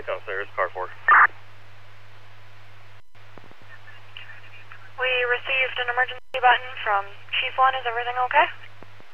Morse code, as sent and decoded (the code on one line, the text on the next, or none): none